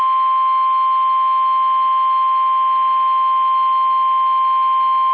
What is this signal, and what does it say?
Emergency Alert System (SAME) tones (the 1050 Hz attention tone). It is heard from 0.8 to 4.9 s.
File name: eas / KUSO / KUSO